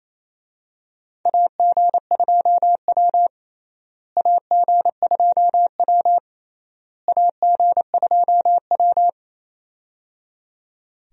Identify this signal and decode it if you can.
Morse code, heard as a keyed tone, about 28 wpm, 700 Hz. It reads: AG2W AG2W AG2W
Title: USA Call Signs - 4 Characters - Repeated 3x - 28wpm